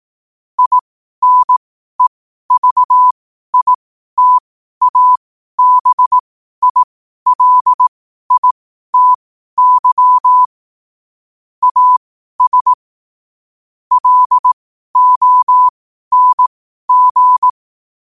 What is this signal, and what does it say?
Morse code, heard as a keyed tone, about 18 wpm, spaced as for 12 wpm, 1000 Hz. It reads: INEVITABILITY AS LONG
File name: qotdfiles / 12wpm